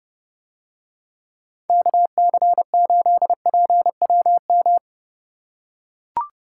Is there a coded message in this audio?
Yes: Morse code from 1.7 to 4.8 s, heard as a keyed tone, about 29 wpm, 700 Hz. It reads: KC8PWM